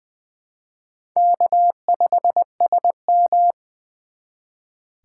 Morse code, heard as a keyed tone, about 24 wpm, 700 Hz. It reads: K5SM
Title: Speed Racing - USA 1x2 Call Signs 20wpm